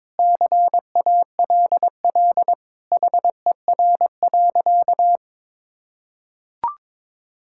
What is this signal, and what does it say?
Morse code, heard as a keyed tone, about 22 wpm, 700 Hz. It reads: CALL HER.
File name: practice / 22wpm / Sentences with 4-Word Phrases - 22wpm